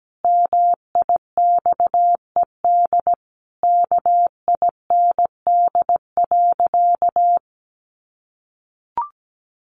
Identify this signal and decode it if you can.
Morse code, heard as a keyed tone, about 17 wpm, 700 Hz. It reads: MIXED KIND.
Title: Sentences from Top 1500 Words 17wpm